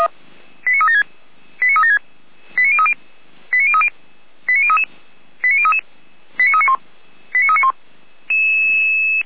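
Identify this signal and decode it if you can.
Five-tone selective call (ZVEI), heard as quick runs of five tones, tones of about 70 ms. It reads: 89376 89376 89030 89030 89033 89033 89391 89391
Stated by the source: Received on an SDR as a ZVEI five-tone call.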